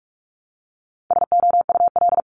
Morse code, heard as a keyed tone, about 46 wpm, 700 Hz. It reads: SOUL
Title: Multi-letter Phonemes Repeated 3x 45wpm